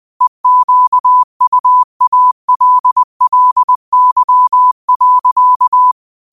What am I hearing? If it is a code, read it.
Morse code, heard as a keyed tone, about 20 wpm, 1000 Hz. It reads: EQUALLY.